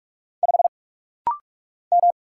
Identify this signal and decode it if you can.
Morse code, heard as a keyed tone, about 44 wpm, 700 Hz. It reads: 5 M
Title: Single Letter-Number Alt - Rapid-Fire 45wpm